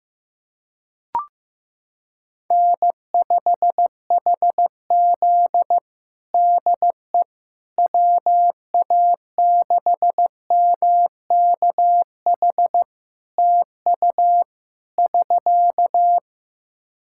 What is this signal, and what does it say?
Morse code, heard as a keyed tone, about 15 wpm, 700 Hz. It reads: N5HZ DE WA6MKH TU <SK>
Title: Practice QSO - Exchange 4 - Ending - 15wpm